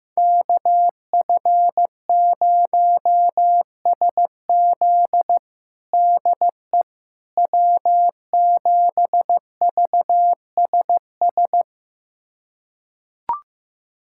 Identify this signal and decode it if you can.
Morse code, heard as a keyed tone, about 15 wpm, 700 Hz. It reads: KF0SZ DE W7VSS
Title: USA Callsign DE Callsign - 5 Characters - 15wpm